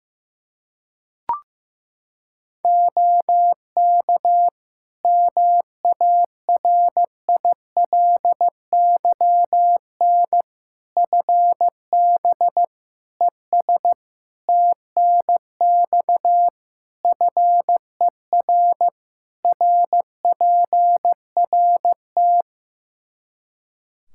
Morse code, heard as a keyed tone, about 15 wpm, 700 Hz. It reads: OK MARILYN FB ES TNX FER RPRT
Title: Practice QSO - Exchange 2 - Info - 15wpm